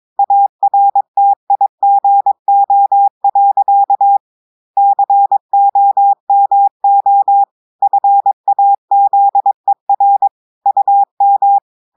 Morse code, heard as a keyed tone, about 22 wpm, 800 Hz. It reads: ARTIGO. COMO FAZER UM
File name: audios / portugais / portugais texte en clair 2 vitesse 22 mots minutes